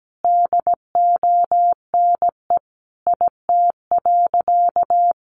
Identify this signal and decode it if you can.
Morse code, heard as a keyed tone, about 17 wpm, 700 Hz. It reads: DONE IT.
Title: Sentences from Top 1200 Words 17wpm